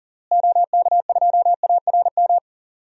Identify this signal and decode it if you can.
Morse code, heard as a keyed tone, about 40 wpm, 700 Hz. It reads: OK2ARM
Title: DXCC-100 Call Signs - 6 Characters - Repeated 3x - 40wpm